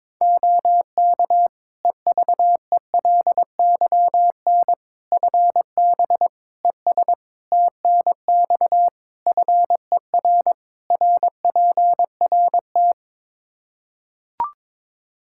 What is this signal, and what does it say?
Morse code, heard as a keyed tone, about 22 wpm, 700 Hz. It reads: OK EVELYN FB ES TNX FER RPRT
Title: Practice QSO - Exchange 2 - Info - 22wpm